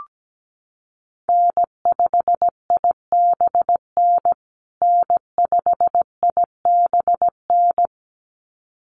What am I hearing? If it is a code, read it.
Morse code, heard as a keyed tone, about 17 wpm, 700 Hz. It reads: N5IBN N5IBN
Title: USA Call Signs Repeated 2x 17wpm